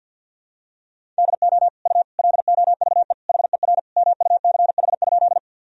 Morse code, heard as a keyed tone, about 46 wpm, 700 Hz. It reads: DO U LOVE HER MUCH?
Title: Sentences from Top 400 Words and Abbreviations 50wpm